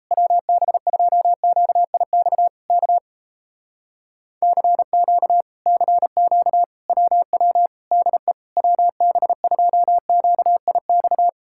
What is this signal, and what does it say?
Morse code, heard as a keyed tone, about 38 wpm, 700 Hz. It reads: WB2QIX K CQ CQ WW DE WB2QIX